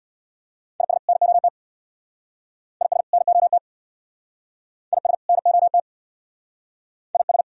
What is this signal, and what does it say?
Morse code, heard as a keyed tone, about 60 wpm, 700 Hz. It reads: IS NOT IS NOT IS NOT IS
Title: Speed-Racing - Sets of 2 Words 50wpm